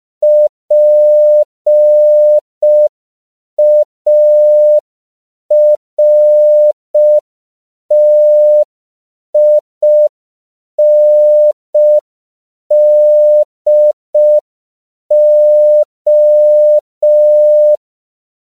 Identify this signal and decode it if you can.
Morse code, heard as a keyed tone, about 5 wpm, 600 Hz. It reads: PARTINDO
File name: CW / 5PPM